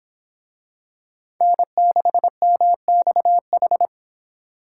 Morse code, heard as a keyed tone, about 25 wpm, 700 Hz. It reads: N6MXH